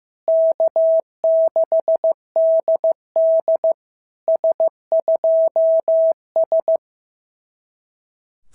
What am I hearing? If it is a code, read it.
Morse code, heard as a keyed tone, about 14 wpm, 650 Hz. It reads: K6DD S2S
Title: SOTA QSO Practice 15wpm